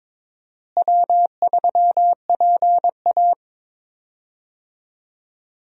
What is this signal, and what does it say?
Morse code, heard as a keyed tone, about 22 wpm, 700 Hz. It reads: W3PA
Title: USA Call Signs - 4 Characters - Repeated 3x - 22wpm